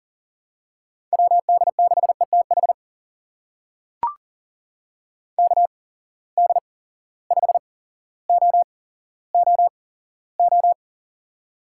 Morse code, heard as a keyed tone, about 41 wpm, spaced as for 17 wpm, 700 Hz. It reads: WD6ETH KD5OOO
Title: USA Call Signs - 6 Characters - 8x Character Spacing - 40wpm